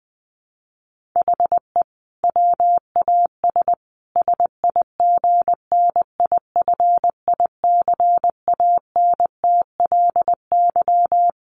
Morse code, heard as a keyed tone, about 20 wpm, 700 Hz. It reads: HE WAS SIGNIFICANTLY
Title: Sentences from Top 1500 Words 20wpm